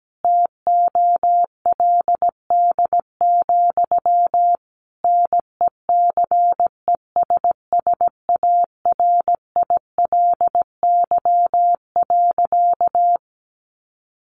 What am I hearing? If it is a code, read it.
Morse code, heard as a keyed tone, about 17 wpm, 700 Hz. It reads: TOLD, NECESSARILY.
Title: Sentences from Top 1400 Words 17wpm